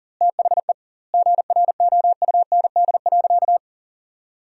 Morse code, heard as a keyed tone, about 40 wpm, 700 Hz. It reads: THE GROUND.
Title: 2x Word-Spacing - Sentences from Top 500 Words - 40wpm